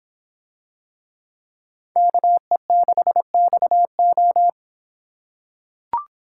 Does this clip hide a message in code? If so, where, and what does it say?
Yes: Morse code from 2.0 to 4.5 s, heard as a keyed tone, about 25 wpm, 700 Hz. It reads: KE6XO